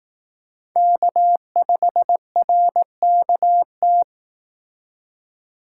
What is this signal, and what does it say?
Morse code, heard as a keyed tone, about 18 wpm, 700 Hz. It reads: K5RKT
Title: Speed Racing - USA 1x3 Call Signs 15wpm